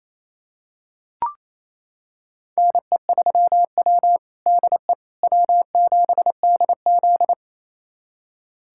Morse code, heard as a keyed tone, about 28 wpm, 700 Hz. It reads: NE3W DE W7DZ
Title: USA Callsign DE Callsign - 4 Characters - 28wpm